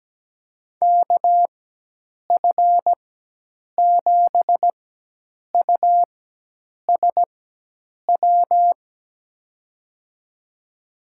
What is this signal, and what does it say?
Morse code, heard as a keyed tone, about 17 wpm, spaced as for 8 wpm, 700 Hz. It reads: KF7USW
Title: USA Call Signs - 6 Characters - 4x Character Spacing - 17wpm